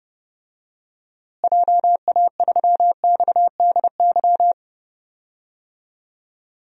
Morse code, heard as a keyed tone, about 32 wpm, 700 Hz. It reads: JA3XDY
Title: DXCC-100 Call Signs - 6 Characters - Repeated 2x - 30wpm